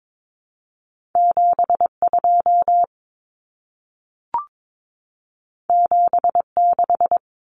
Morse code, heard as a keyed tone, about 22 wpm, 700 Hz. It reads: 72 76
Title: Common Cut Numbers - 2-Digit 22wpm